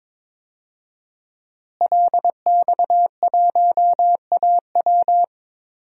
Morse code, heard as a keyed tone, about 22 wpm, 700 Hz. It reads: LX1AW
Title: DXCC-100 Call Signs - 5 Characters - Repeated 3x - 22wpm